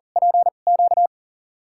Morse code, heard as a keyed tone, about 40 wpm, spaced as for 29 wpm, 690 Hz. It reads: PQ
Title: USA Call Signs - 2x Character Spacing - 40wpm